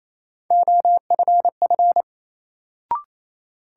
Morse code, heard as a keyed tone, about 28 wpm, 700 Hz. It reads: OFF
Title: Sentences from Top 200 Words - New Word Drill - 28wpm